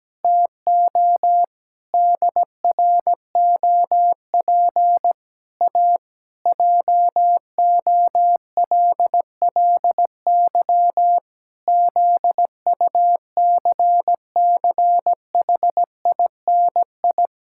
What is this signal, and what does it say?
Morse code, heard as a keyed tone, about 17 wpm, 700 Hz. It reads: TO DROP A JOLLY ZUCCHINI
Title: Pangram Challenge v2 - 17wpm